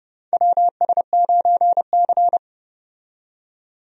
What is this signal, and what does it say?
Morse code, heard as a keyed tone, about 30 wpm, 700 Hz. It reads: WS9C
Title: USA Call Signs - 4 Characters - 30wpm